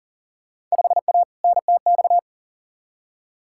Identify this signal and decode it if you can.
Morse code, heard as a keyed tone, about 40 wpm, 700 Hz. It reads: 5A NTX
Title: Field Day Exchanges - 40wpm